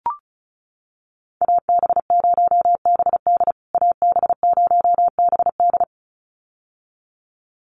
Morse code, heard as a keyed tone, about 34 wpm, 700 Hz. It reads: AB0BD AB0BD
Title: USA Call Signs Repeated 2x 35wpm